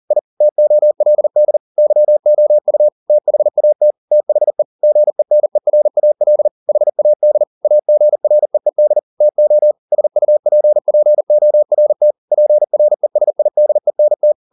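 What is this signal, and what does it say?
Morse code, heard as a keyed tone, about 40 wpm, 600 Hz. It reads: I TOLD YOU THAT THE GENERAL HAD AGREED TO SUPPORT PRESIDENT
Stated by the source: Morse code at 40 wpm, practice text, CW med Gnister.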